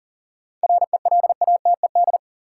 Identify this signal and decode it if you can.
Morse code, heard as a keyed tone, about 40 wpm, 700 Hz. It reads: RELATED
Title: Sentences from Top 1000 Words - Review Part 5 - New Word Drill - 40wpm